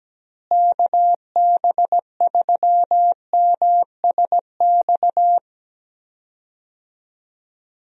Morse code, heard as a keyed tone, about 17 wpm, 700 Hz. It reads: KB3MSX
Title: USA Call Signs - 6 Characters - Repeated 3x - 17wpm